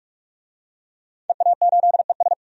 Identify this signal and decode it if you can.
Morse code, heard as a keyed tone, about 41 wpm, 700 Hz. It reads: EA8ES
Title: DXCC-100 Call Signs - 5 Characters - 45wpm